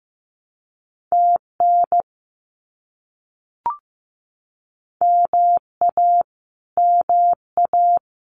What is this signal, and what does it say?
Morse code, heard as a keyed tone, about 15 wpm, 700 Hz. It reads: TN MA MA
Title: US States Repeated 3x 15wpm